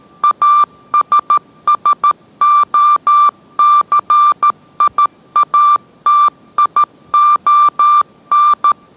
Morse code, heard as a keyed tone, about 13 wpm, 1180 Hz. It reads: ASSOCIATION